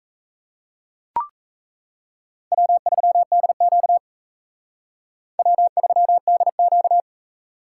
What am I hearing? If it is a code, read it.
Morse code, heard as a keyed tone, about 35 wpm, 700 Hz. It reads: W3DQ W3DQ
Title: Speed Racing - USA 1x2 Call Signs 28wpm